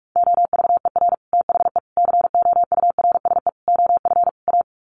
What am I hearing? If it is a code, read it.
Morse code, heard as a keyed tone, about 44 wpm, 700 Hz. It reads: OVER THE COURSE OF A
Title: Sets of 5 Words Encore 45wpm